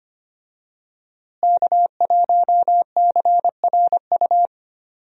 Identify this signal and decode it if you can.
Morse code, heard as a keyed tone, about 25 wpm, 700 Hz. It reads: K1CRU
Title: USA Call Signs - 5 Characters - Repeated 3x - 25wpm